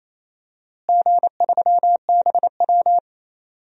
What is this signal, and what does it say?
Morse code, heard as a keyed tone, about 28 wpm, 700 Hz. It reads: G3BW
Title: DXCC-100 Call Signs - 4 Characters - Repeated 3x - 28wpm